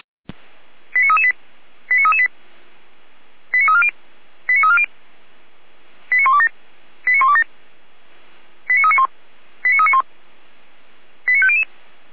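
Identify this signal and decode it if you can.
Five-tone selective call (ZVEI), heard as quick runs of five tones, tones of about 70 ms. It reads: 89338 89338 89340 89340 89127 89127 89391 89391 89500
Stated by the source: SDR-received ZVEI five-tone selective call, code 89320.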